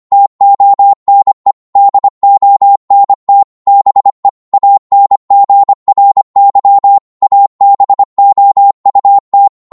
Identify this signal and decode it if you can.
Morse code, heard as a keyed tone, about 25 wpm, 800 Hz. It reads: TONE DONT BE ANGRY ABOUT